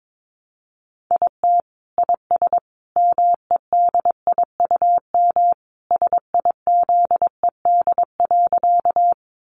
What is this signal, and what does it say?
Morse code, heard as a keyed tone, about 22 wpm, 700 Hz. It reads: IT IS MEDIUM SIZED.
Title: Easy Phrases 22wpm